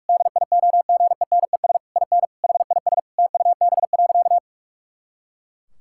Morse code, heard as a keyed tone, about 46 wpm, 700 Hz. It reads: DIOGENES IN HIS TUB.
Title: Pangram Challenge v2 - 45wpm